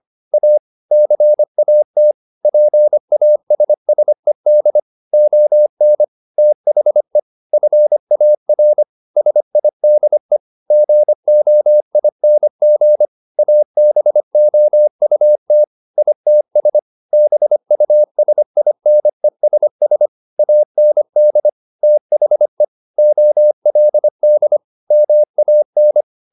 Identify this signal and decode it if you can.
Morse code, heard as a keyed tone, about 25 wpm, 600 Hz. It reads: A CAT PASSED ON THE FAR SIDE GOING ABOUT ITS BUSINESS AND THE OLD MAN